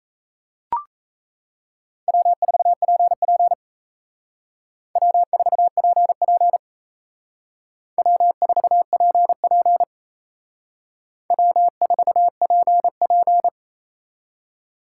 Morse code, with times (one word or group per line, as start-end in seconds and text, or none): 2.1-3.5 W4PP
5.0-6.6 W4PP
8.0-9.8 W4PP
11.3-13.5 W4PP